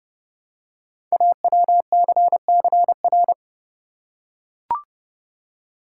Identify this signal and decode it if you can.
Morse code, heard as a keyed tone, about 30 wpm, 700 Hz. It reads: AWCCR
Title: Code Groups - 5 Letters - 30wpm